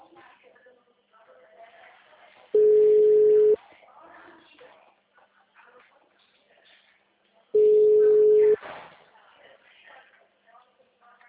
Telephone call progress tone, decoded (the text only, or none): ringback tone